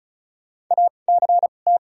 Morse code, heard as a keyed tone, about 34 wpm, spaced as for 25 wpm, 700 Hz. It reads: ACT
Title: 2x Character Spacing - Top 200 Words - 35wpm